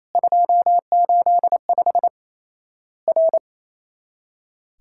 Morse code, heard as a keyed tone, about 28 wpm, 700 Hz. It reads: 285 R
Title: Worked All Europe DX Contest - QTC Exchange - 28wpm